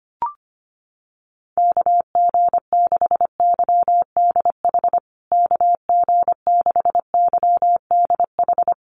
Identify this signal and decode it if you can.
Morse code, heard as a keyed tone, about 25 wpm, 700 Hz. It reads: KG6YDH KG6YDH
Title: USA Call Signs Repeated 3x 25wpm